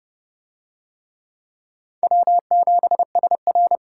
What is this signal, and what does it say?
Morse code, heard as a keyed tone, about 28 wpm, 690 Hz. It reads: W7SR